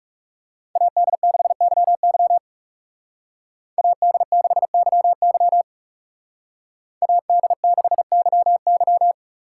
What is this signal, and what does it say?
Morse code, heard as a keyed tone, about 35 wpm, 700 Hz. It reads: AD6YY AD6YY AD6YY
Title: Speed Racing - USA 2x2 Call Signs 30wpm